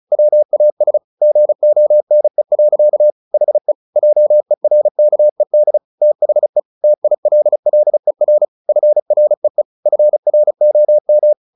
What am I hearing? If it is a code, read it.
Morse code, heard as a keyed tone, about 35 wpm, 600 Hz. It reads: WAS GONE. HE JERKED THE TILLER FREE FROM
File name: the old man and the sea 2 35wpm